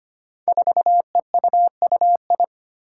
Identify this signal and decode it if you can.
Morse code, heard as a keyed tone, about 25 wpm, 700 Hz. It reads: 4EUUI
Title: Five Challenging Letter-Number Combinations 25wpm